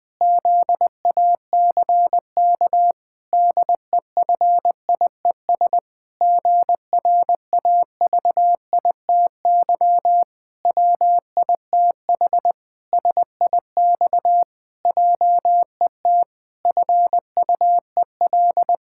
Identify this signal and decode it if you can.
Morse code, heard as a keyed tone, about 20 wpm, 700 Hz. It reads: ZACK DEFIES GRAVITY WITH SIX JET FUEL
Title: Pangram Challenge v2 - 20wpm